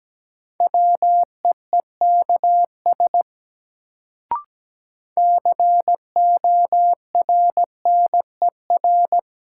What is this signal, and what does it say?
Morse code, heard as a keyed tone, about 17 wpm, 700 Hz. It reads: WEEKS CORNER
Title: Sets of 5 Words - New Word Drill - 17wpm